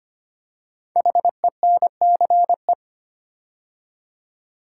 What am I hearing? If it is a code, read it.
Morse code, heard as a keyed tone, about 25 wpm, 700 Hz. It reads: HENCE
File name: 5 Letter Words - First 3 Letters Given - 25wpm